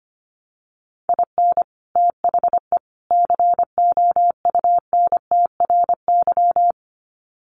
Morse code, heard as a keyed tone, about 25 wpm, 700 Hz. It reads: IN THE COUNTRY
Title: Sets of 3 Words 25wpm